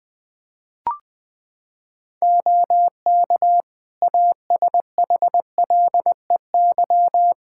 Morse code, heard as a keyed tone, about 20 wpm, 700 Hz. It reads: OK ASHLEY
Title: Practice QSO - Exchange 4 - Ending - 20wpm